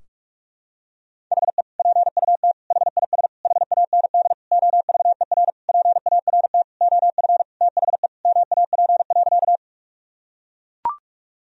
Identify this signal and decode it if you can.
Morse code, heard as a keyed tone, about 41 wpm, 700 Hz. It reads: HE PUT HIS HAND OVER PART OF THE MAP.
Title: Sentences from Top 1000 Words - Review Part 2 - 45wpm